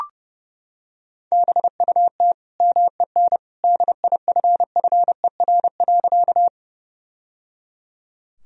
Morse code, heard as a keyed tone, about 30 wpm, 700 Hz. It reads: BUT MEN DIFFER.
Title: Sentences from Top 1000 Words - Review Part 1 - 30wpm